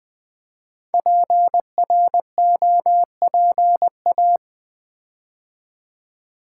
Morse code, heard as a keyed tone, about 20 wpm, 700 Hz. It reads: PROPA
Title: Build A Word - 5 Letters At A Time - 10-Letter Words - 20wpm